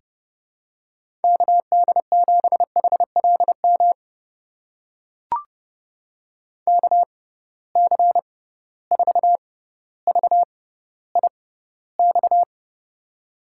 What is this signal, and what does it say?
Morse code, heard as a keyed tone, about 31 wpm, spaced as for 15 wpm, 700 Hz. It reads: KD7HLM KC4VIX
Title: USA Call Signs - 6 Characters - 6x Character Spacing - 30wpm